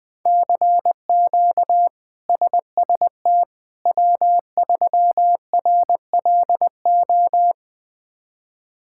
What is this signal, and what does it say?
Morse code, heard as a keyed tone, about 20 wpm, 700 Hz. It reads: CQ SST W3RLO
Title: K1USN SST QSO Practice - 20wpm